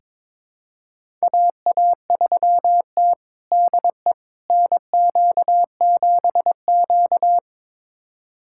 Morse code, heard as a keyed tone, about 21 wpm, 700 Hz. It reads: AA3T DE NQ7Q
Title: USA Callsign DE Callsign - 4 Characters - 22wpm